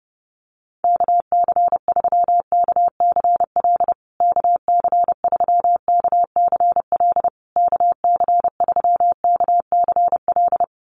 Morse code, heard as a keyed tone, about 30 wpm, 700 Hz. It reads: KC3KCL KC3KCL KC3KCL
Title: USA Call Signs Repeated 3x 30wpm